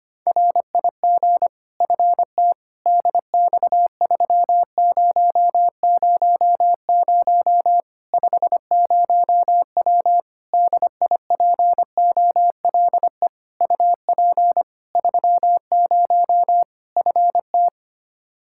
Morse code, heard as a keyed tone, about 25 wpm, 700 Hz. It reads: RIG FT DX3000 50W DIPOLE UP 30 FT
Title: Practice QSO - Exchange 2 - Info - 25wpm